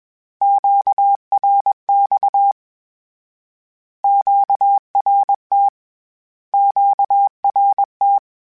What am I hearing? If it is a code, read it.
Morse code, heard as a keyed tone, about 21 wpm, 800 Hz. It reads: QRX QRT QRT
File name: Q Calls / Q Calls-mp3s / Q Calls Twice @20